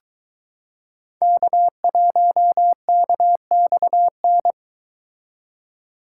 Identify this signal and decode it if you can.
Morse code, heard as a keyed tone, about 23 wpm, 700 Hz. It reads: K1KXN